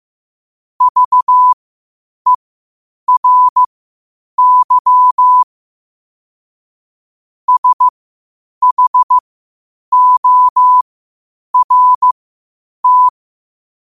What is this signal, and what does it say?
Morse code, heard as a keyed tone, about 15 wpm, spaced as for 8 wpm, 1000 Hz. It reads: VERY SHORT
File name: qotdfiles / 8wpm